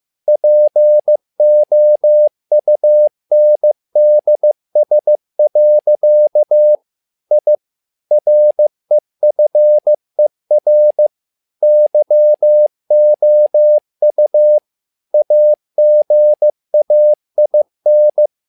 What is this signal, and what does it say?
Morse code, heard as a keyed tone, about 15 wpm, 600 Hz. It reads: POUNDS. I REFER YOU AGAIN
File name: learning center / tower trouble / tower trouble 15wpm